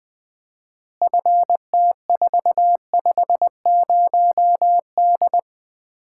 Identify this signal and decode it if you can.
Morse code, heard as a keyed tone, about 20 wpm, 700 Hz. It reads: FT450D
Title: Common QSO Phrases - 4x Word-Spacing - 20wpm